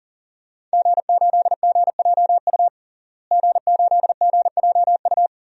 Morse code, heard as a keyed tone, about 40 wpm, 700 Hz. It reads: G8GJU G8GJU
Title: DXCC-100 Call Signs - 5 Characters - Repeated 2x - 40wpm